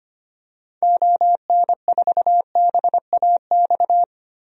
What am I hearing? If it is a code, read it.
Morse code, heard as a keyed tone, about 25 wpm, 700 Hz. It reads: ON4BAX
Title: DXCC-100 Call Signs - 6 Characters - Repeated 2x - 25wpm